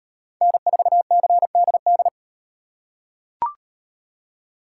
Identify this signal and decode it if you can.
Morse code, heard as a keyed tone, about 38 wpm, 700 Hz. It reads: N4CDD